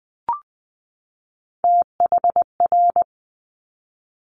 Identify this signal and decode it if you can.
Morse code, heard as a keyed tone, about 20 wpm, 700 Hz. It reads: THR
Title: Top Three Letter Combinations in Words 20wpm